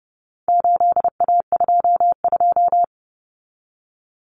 Morse code, heard as a keyed tone, about 30 wpm, 700 Hz. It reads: 8A22
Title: Common Cut Numbers - 4-Digit 30wpm